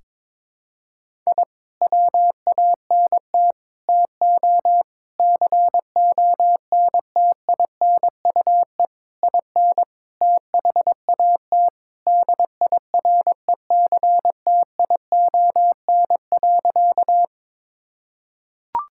Morse code, heard as a keyed tone, about 22 wpm, 700 Hz. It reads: I WANT TO CONTINUE IN THAT DIRECTION.
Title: Sentences from Top 1000 Words - Review Part 6 - 22wpm